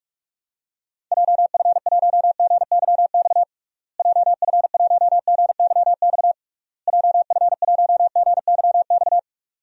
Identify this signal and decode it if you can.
Morse code, heard as a keyed tone, about 41 wpm, 700 Hz. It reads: JF1GYX JF1GYX JF1GYX
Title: DXCC-100 Call Signs - 6 Characters - Repeated 3x - 45wpm